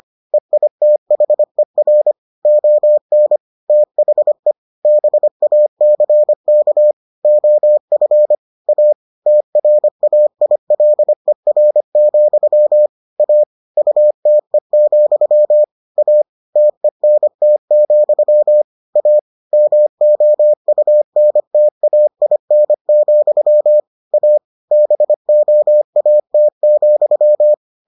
Morse code, heard as a keyed tone, about 25 wpm, 600 Hz. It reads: EITHER ON THE BACK OF A TRAILER, A UTE, A TENT, A MOUNTAIN, A BOAT,